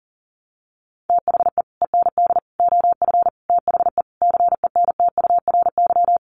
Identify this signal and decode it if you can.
Morse code, heard as a keyed tone, about 40 wpm, 700 Hz. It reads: THE END OF THE CENTURY
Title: Sets of 5 Words Encore2 40wpm